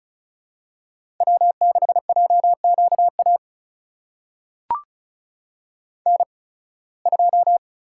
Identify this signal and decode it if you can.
Morse code, heard as a keyed tone, about 34 wpm, spaced as for 23 wpm, 700 Hz. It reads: W6JQA N2
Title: USA Call Signs - 5 Characters - 8x Character Spacing - 35wpm